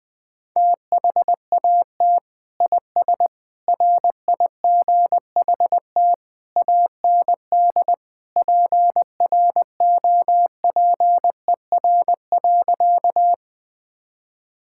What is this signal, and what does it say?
Morse code, heard as a keyed tone, about 20 wpm, 700 Hz. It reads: THAT IS RIGHT AND PROPER.
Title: Sentences from Top 1000 Words - Review Part 5 - 20wpm